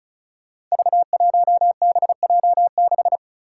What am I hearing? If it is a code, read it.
Morse code, heard as a keyed tone, about 34 wpm, 700 Hz. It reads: V1BJ6
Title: Five Challenging Letter-Number Combinations 35wpm